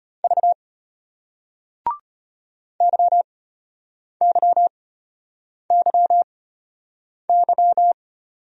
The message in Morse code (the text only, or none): V YYYY